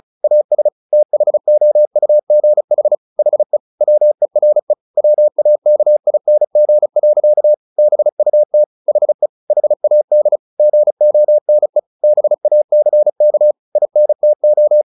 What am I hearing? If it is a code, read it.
Morse code, heard as a keyed tone, about 34 wpm, 600 Hz. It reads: AS THOUGH HE WERE WAKING. BUT HE HAD GONE BACK INTO
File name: the old man and the sea 2 35wpm